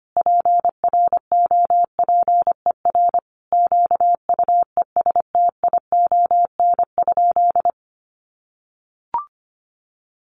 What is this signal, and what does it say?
Morse code, heard as a keyed tone, about 25 wpm, 700 Hz. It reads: PROPER QUESTION?
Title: Sentences from Top 1100 Words 25wpm